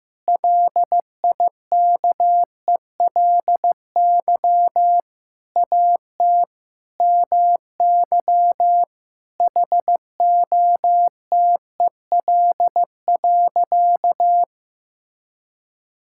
Morse code, heard as a keyed tone, about 15 wpm, 700 Hz. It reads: LIKELY AT MY HOTEL.
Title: Sentences from Top 1600 Words 15wpm